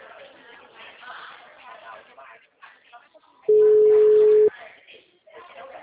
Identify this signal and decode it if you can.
Ringback tone (Europe, 425 Hz), with speech.
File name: out- -2010- - - .18146